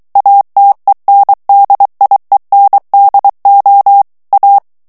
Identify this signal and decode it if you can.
Morse code, heard as a keyed tone, about 23 wpm, 800 Hz. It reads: ATENDIENDO A
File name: morse mp3 / ESPAGNOL TEXTE EN CLAIR 2 VITESSE 22 MOTS MINUTES